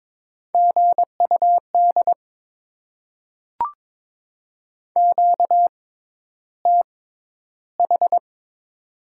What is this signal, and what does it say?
Morse code, heard as a keyed tone, about 22 wpm, spaced as for 11 wpm, 690 Hz. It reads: GUD QTH
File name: Top 20 QSO Elements - 6x Character Spacing - 22wpm